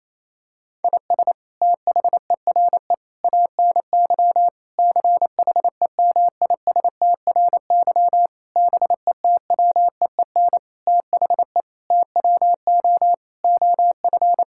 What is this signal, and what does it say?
Morse code, heard as a keyed tone, about 28 wpm, 700 Hz. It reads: IS THERE ANY CHEMISTRY BETWEEN THE TWO OF
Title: Sentences from Top 1000 Words - Review Part 6 - 28wpm